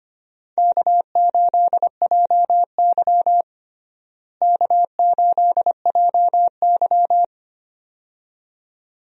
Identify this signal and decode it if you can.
Morse code, heard as a keyed tone, about 25 wpm, 700 Hz. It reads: K8JY K8JY
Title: USA Call Signs - 4 Characters - Repeated 2x - 25wpm